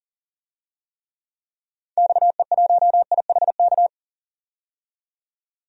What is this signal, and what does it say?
Morse code, heard as a keyed tone, about 40 wpm, 700 Hz. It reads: XE1IHK
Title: DXCC-100 Call Signs - 6 Characters - 40wpm